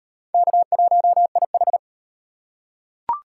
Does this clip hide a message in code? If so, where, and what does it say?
Yes: Morse code from 0.3 to 1.8 s, heard as a keyed tone, about 38 wpm, 700 Hz. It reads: K1IH